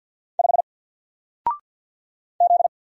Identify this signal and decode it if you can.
Morse code, heard as a keyed tone, about 49 wpm, 700 Hz. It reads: 5 Z
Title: Single Letter-Number 50wpm